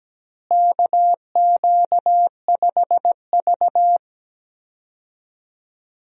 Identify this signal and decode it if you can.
Morse code, heard as a keyed tone, about 17 wpm, 700 Hz. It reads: KQ5V
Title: USA Call Signs - 4 Characters - Repeated 2x - 17wpm